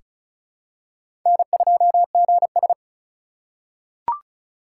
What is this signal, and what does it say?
Morse code, heard as a keyed tone, about 36 wpm, 700 Hz. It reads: N2GS
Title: USA Call Signs - 4 Characters - 4x Character Spacing - 35wpm